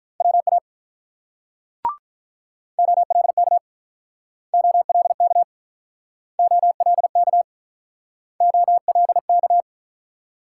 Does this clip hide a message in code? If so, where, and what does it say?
Yes: Morse code from 0.2 to 9.6 s, heard as a keyed tone, about 48 wpm, 700 Hz. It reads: WA OLK OLK OLK OLK